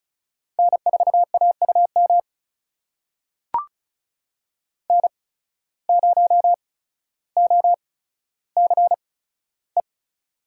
Morse code, heard as a keyed tone, about 36 wpm, spaced as for 15 wpm, 700 Hz. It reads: N4AUM N0OCE
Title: USA Call Signs - 5 Characters - 8x Character Spacing - 35wpm